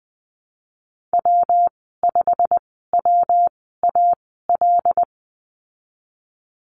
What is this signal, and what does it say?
Morse code, heard as a keyed tone, about 20 wpm, spaced as for 15 wpm, 700 Hz. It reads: W5WAL
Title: 186 Lesson 30g - Letter b - Call Signs 20wpm